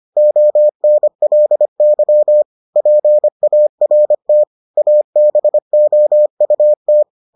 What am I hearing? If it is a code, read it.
Morse code, heard as a keyed tone, about 25 wpm, 600 Hz. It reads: ONLY PART ABOUT